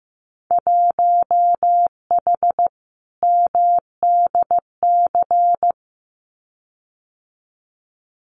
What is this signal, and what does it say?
Morse code, heard as a keyed tone, about 15 wpm, 700 Hz. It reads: 1H MDC
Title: Winter Field Day Exchanges 15wpm